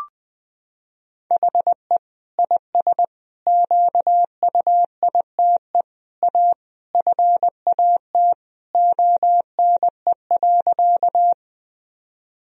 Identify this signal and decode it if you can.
Morse code, heard as a keyed tone, about 20 wpm, 700 Hz. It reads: HE IS QUITE A FAT ONE.
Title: Sentences from Top 1000 Words - Review Part 4 - 20wpm